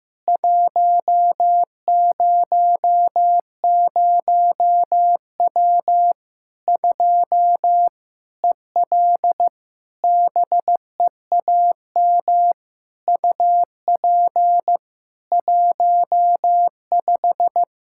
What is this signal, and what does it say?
Morse code, heard as a keyed tone, about 15 wpm, 700 Hz. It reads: 100W 2 EL BEAM UP 15
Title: Practice QSO - Exchange 2 - Info - 15wpm